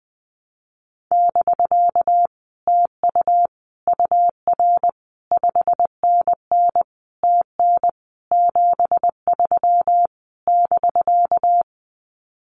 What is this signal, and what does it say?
Morse code, heard as a keyed tone, about 19 wpm, 700 Hz. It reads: <BK> TU UR 5NN TN 73 <BK>
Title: Common QSO Phrases Encore 20wpm